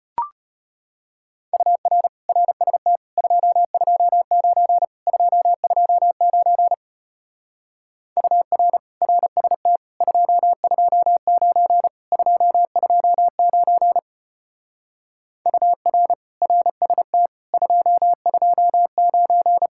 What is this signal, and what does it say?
Morse code, heard as a keyed tone, about 32 wpm, 700 Hz. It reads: UR RST 229 229 UR RST 229 229 UR RST 229